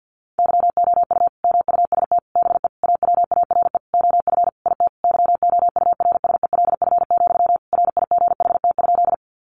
Morse code, heard as a keyed tone, about 50 wpm, 700 Hz. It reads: YOU MUST BE AWARE OF IT YOURSELF, RIGHT?
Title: Sentences from Top 1200 Words 50wpm